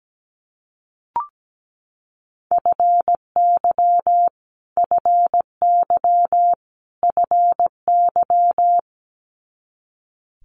Morse code, heard as a keyed tone, about 17 wpm, 700 Hz. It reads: FY FY FY
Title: Multi-letter Phonemes Repeated 3x 17wpm